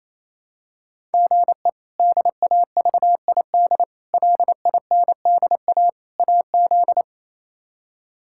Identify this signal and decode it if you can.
Morse code, heard as a keyed tone, about 28 wpm, 700 Hz. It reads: GE DAVID LINDA AZ
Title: K1USN SST QSO Practice - 28wpm